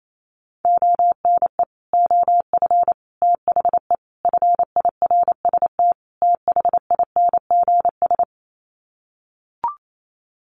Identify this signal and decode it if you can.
Morse code, heard as a keyed tone, about 28 wpm, 700 Hz. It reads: ONE OF THE FIRST THINGS
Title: Sets of 5 Words Encore 28wpm